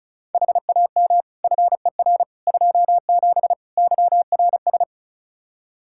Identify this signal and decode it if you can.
Morse code, heard as a keyed tone, about 34 wpm, 700 Hz. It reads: HAM FER 27 YRS
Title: Practice QSO - Exchange 3 - More Info - 35wpm